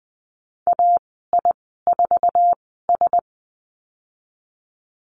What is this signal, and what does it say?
Morse code, heard as a keyed tone, about 20 wpm, spaced as for 15 wpm, 700 Hz. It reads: AI4S
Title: 055 Lesson 10g - Letter r - Call Signs 20wpm